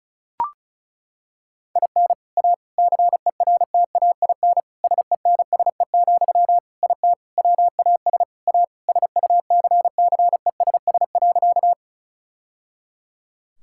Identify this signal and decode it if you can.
Morse code, heard as a keyed tone, about 36 wpm, 700 Hz. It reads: IN A CERTAIN SENSE, IT WAS A SUCCESS.
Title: Sentences from Top 1000 Words - Review Part 6 - 35wpm